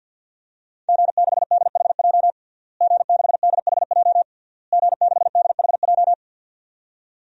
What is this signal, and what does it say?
Morse code, heard as a keyed tone, about 51 wpm, 700 Hz. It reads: G6DHJ G6DHJ G6DHJ
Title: DXCC-100 Call Signs - 5 Characters - Repeated 3x - 50wpm